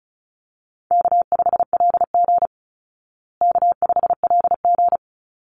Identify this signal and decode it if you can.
Morse code, heard as a keyed tone, about 34 wpm, 700 Hz. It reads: K5LG K5LG
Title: Call Signs Repeated 3x - 35wpm